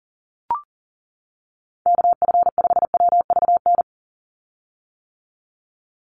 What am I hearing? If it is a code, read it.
Morse code, heard as a keyed tone, about 40 wpm, 690 Hz. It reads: KF5WVN
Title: USA Call Signs 40wpm